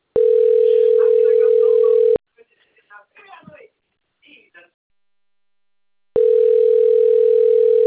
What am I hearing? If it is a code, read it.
Ringback tone (North America), 440 + 480 Hz, with speech.
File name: rg-1600-unknown- - - .33631